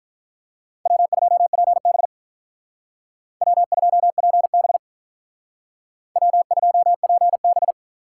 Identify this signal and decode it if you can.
Morse code, heard as a keyed tone, about 46 wpm, 700 Hz. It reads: W2PB W2PB W2PB